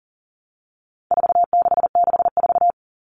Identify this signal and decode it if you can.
Morse code, heard as a keyed tone, about 40 wpm, 700 Hz. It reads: 4664
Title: All Cut Numbers - 4-Digit 40wpm